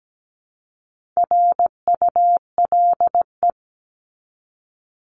Morse code, heard as a keyed tone, about 17 wpm, 700 Hz. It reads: RULE
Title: ES:S Suffix - Top 100 Words with Base Words 17wpm